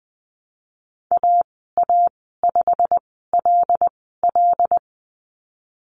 Morse code, heard as a keyed tone, about 20 wpm, spaced as for 15 wpm, 700 Hz. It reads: AA5LL
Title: 088 Lesson 15g - Number 5 - Call Signs 20wpm